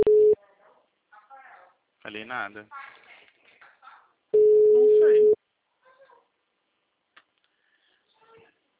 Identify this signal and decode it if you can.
Ringback tone (Europe, 425 Hz), with speech.